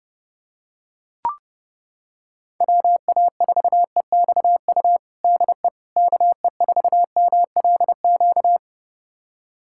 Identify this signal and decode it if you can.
Morse code, heard as a keyed tone, about 30 wpm, 700 Hz. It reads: WA4EXU DE KE4MLQ
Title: USA Callsign DE Callsign - 6 Characters - 30wpm